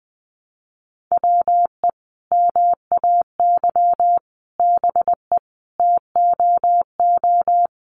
Morse code, heard as a keyed tone, about 20 wpm, 700 Hz. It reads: WE MAY BE TOO
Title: Sentences from Top 300 Words 20wpm